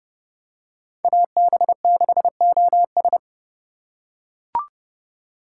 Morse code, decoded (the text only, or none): AB6OS